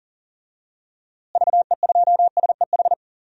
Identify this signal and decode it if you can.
Morse code, heard as a keyed tone, about 40 wpm, 700 Hz. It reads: VE2SEH
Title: DXCC-100 Call Signs - 6 Characters - Repeated 2x - 40wpm